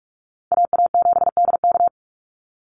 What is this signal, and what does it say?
Morse code, heard as a keyed tone, about 44 wpm, 690 Hz. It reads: AA7DK
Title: NA SOTA Chaser Callsigns 45wpm